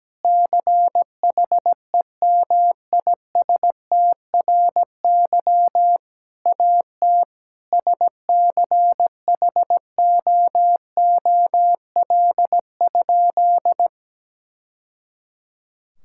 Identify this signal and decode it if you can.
Morse code, heard as a keyed tone, about 17 wpm, 700 Hz. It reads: CHEMISTRY AT SCHOOL?
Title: Sentences from Top 1000 Words - Review Part 6 - 17wpm